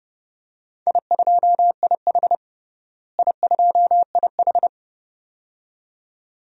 Morse code, heard as a keyed tone, about 30 wpm, 700 Hz. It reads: I2IH I2IH
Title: DXCC-100 Call Signs - 4 Characters - Repeated 2x - 30wpm